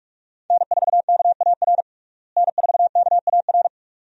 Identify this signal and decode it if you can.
Morse code, heard as a keyed tone, about 42 wpm, 700 Hz. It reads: N4KAR N4KAR
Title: USA Call Signs - 5 Characters - Repeated 2x - 45wpm